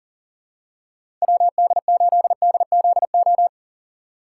Morse code, heard as a keyed tone, about 37 wpm, 700 Hz. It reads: WD8DZO
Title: USA Call Signs - 2x Character Spacing - 40wpm